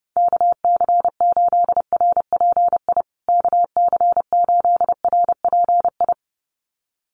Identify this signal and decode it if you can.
Morse code, heard as a keyed tone, about 30 wpm, 700 Hz. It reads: KC8RPI KC8RPI
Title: USA Call Signs Repeated 2x 30wpm